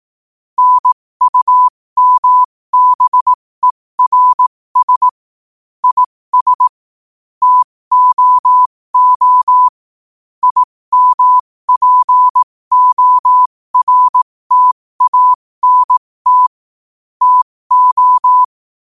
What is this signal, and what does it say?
Morse code, heard as a keyed tone, about 18 wpm, spaced as for 15 wpm, 1000 Hz. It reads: NUMBERS IS TOO IMPORTANT TO
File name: qotdfiles / 15wpm